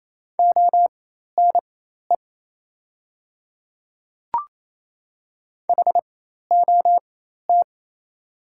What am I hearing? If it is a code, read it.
Morse code, heard as a keyed tone, about 27 wpm, spaced as for 13 wpm, 690 Hz. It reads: ONE HOT
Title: 4x Character Spacing - Top 300 Words - 28wpm